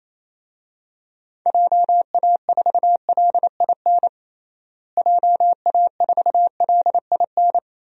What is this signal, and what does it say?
Morse code, heard as a keyed tone, about 28 wpm, 700 Hz. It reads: JA4LIN JA4LIN
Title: DXCC-100 Call Signs - 6 Characters - Repeated 2x - 28wpm